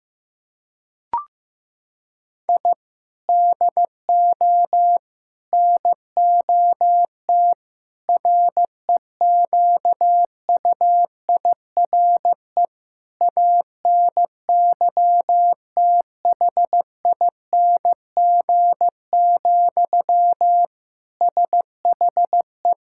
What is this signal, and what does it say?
Morse code, heard as a keyed tone, about 16 wpm, 700 Hz. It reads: I DO NOT REQUIRE ANYTHING, SHE
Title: Sentences from Top 1000 Words - Review Part 6 - 15wpm